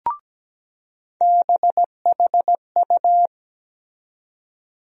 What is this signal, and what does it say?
Morse code, heard as a keyed tone, about 17 wpm, 700 Hz. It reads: BHU
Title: USA Call Sign Postfixes - 3 Letters 17wpm